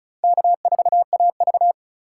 Morse code, heard as a keyed tone, about 36 wpm, 700 Hz. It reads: K4AV
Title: USA Call Signs - 6x Character Spacing - 35wpm